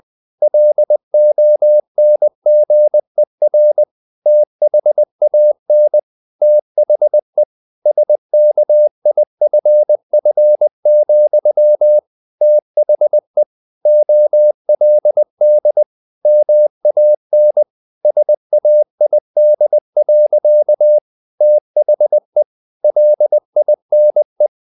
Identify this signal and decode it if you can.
Morse code, heard as a keyed tone, about 20 wpm, 600 Hz. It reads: LONGER THAN THE SKIFF, THE OLD MAN SAID. THE LINE
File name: the old man and the sea / the old man and the sea part 1 20wpm